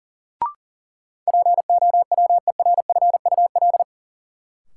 Morse code, heard as a keyed tone, about 37 wpm, 700 Hz. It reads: POWERFUL
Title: Sentences from Top 1000 Words - Review Part 6 - New Word Drill - 40wpm